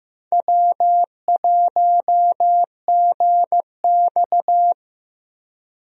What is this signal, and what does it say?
Morse code, heard as a keyed tone, about 16 wpm, 700 Hz. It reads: W1GX
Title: USA Call Signs - 4 Characters - 15wpm